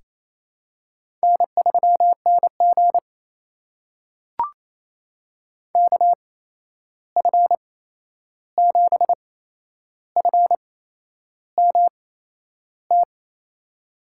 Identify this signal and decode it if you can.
Morse code, heard as a keyed tone, about 28 wpm, spaced as for 11 wpm, 700 Hz. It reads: N3NG KF7FMT
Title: USA Call Signs - 8x Character Spacing - 28wpm